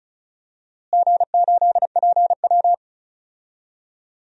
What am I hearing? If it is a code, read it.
Morse code, heard as a keyed tone, about 34 wpm, 700 Hz. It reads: G8PW
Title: DXCC-100 Call Signs - 4 Characters - Repeated 2x - 35wpm